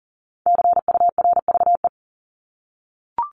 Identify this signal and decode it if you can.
Morse code, heard as a keyed tone, about 40 wpm, 700 Hz. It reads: CURVE
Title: Top 1501-1750 Words 40wpm